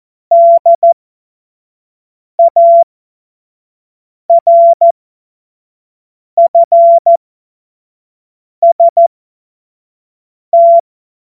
Morse code, heard as a keyed tone, about 14 wpm, spaced as for 5 wpm, 680 Hz. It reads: DARFST
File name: Morsekurs / Witze / Witz 20. Dezember 2022 25 F70